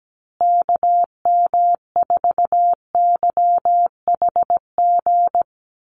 Morse code, heard as a keyed tone, about 17 wpm, 700 Hz. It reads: KM4YHG
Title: USA Call Signs Repeated 2x 17wpm